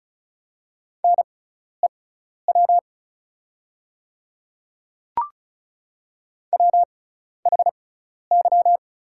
Morse code, heard as a keyed tone, about 35 wpm, spaced as for 12 wpm, 690 Hz. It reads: NEW WHY
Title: 6x Character Spacing - Top 300 Words - 35wpm